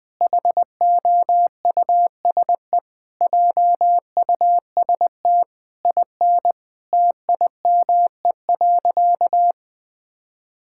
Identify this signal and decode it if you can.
Morse code, heard as a keyed tone, about 20 wpm, 700 Hz. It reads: HOUSE JUST IN TIME.
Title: Sentences from Top 1600 Words 20wpm